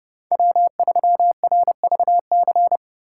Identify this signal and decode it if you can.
Morse code, heard as a keyed tone, about 39 wpm, 700 Hz. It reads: W3RVC